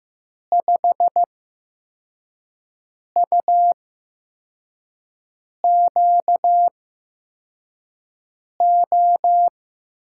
Morse code, heard as a keyed tone, about 15 wpm, spaced as for 4 wpm, 700 Hz. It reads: 5UQO
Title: USA Call Signs - 8x Character Spacing - 15wpm